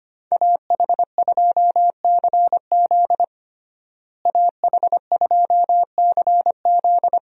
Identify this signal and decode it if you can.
Morse code, heard as a keyed tone, about 25 wpm, 700 Hz. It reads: AH2CZ AH2CZ
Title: USA Call Signs - 5 Characters - Repeated 3x - 25wpm